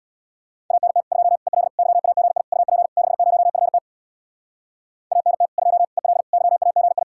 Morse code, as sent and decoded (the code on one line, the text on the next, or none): .- -. - / ..--- / . .-.. / -.-- .- --. .. / ..- .--. / -.... ----- ..-. - / .- -. - / ..--- / . .-.. / -.-- .- --. ..
ANT 2 EL YAGI UP 60FT ANT 2 EL YAGI